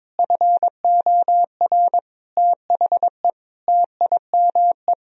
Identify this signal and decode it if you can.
Morse code, heard as a keyed tone, about 22 wpm, 700 Hz. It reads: FOR THE TIME
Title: Sentences from Top 1000 Words - Review Part 3 - 22wpm